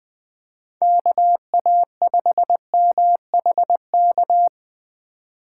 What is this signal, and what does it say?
Morse code, heard as a keyed tone, about 20 wpm, 700 Hz. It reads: KA5MHK
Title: USA Call Signs - 2x Character Spacing - 20wpm